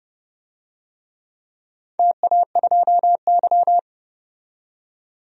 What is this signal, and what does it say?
Morse code, heard as a keyed tone, about 30 wpm, 700 Hz. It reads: TA2Y
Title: DXCC-100 Call Signs - 4 Characters - 30wpm